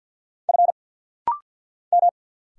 Morse code, heard as a keyed tone, about 50 wpm, 700 Hz. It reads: F M
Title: Single Letters - Rapid-Fire 50wpm